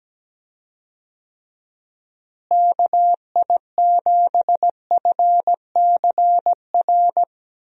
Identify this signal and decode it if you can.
Morse code, heard as a keyed tone, about 17 wpm, 700 Hz. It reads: KI7FCR
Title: USA Call Signs - 6 Characters - Repeated 2x - 17wpm